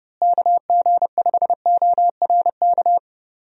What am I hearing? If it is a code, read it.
Morse code, heard as a keyed tone, about 32 wpm, 700 Hz. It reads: KG5ORK
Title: USA Call Signs - 6 Characters - Repeated 2x - 30wpm